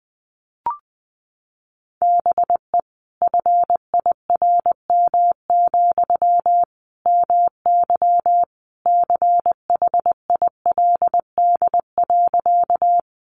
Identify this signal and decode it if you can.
Morse code, heard as a keyed tone, about 20 wpm, 700 Hz. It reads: BE FIRM, MY CHILD.
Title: Sentences from Top 1400 Words 20wpm